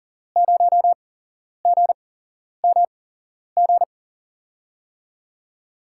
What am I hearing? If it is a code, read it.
Morse code, heard as a keyed tone, about 40 wpm, 700 Hz. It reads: 0 G M G
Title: USA Call Signs - 6 Characters - 8x Character Spacing - 40wpm